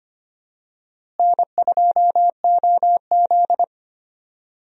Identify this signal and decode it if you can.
Morse code, heard as a keyed tone, about 25 wpm, 700 Hz. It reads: N2OZ
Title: USA Call Signs - 4 Characters - Repeated 2x - 25wpm